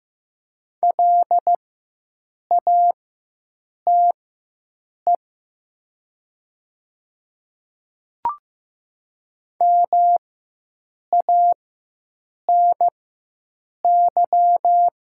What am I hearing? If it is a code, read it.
Morse code, heard as a keyed tone, about 15 wpm, spaced as for 7 wpm, 700 Hz. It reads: LATE MANY
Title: 4x Character Spacing - Top 300 Words - 15wpm